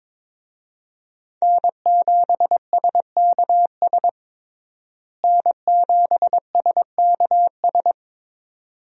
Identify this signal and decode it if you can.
Morse code, heard as a keyed tone, about 22 wpm, 700 Hz. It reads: N7SKS N7SKS
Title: USA Call Signs - 5 Characters - Repeated 2x - 22wpm